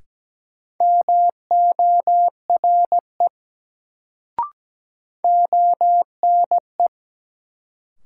Morse code, heard as a keyed tone, about 17 wpm, 700 Hz. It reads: MORE ONE
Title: Sentences from Top 100 Words - New Word Drill - 17wpm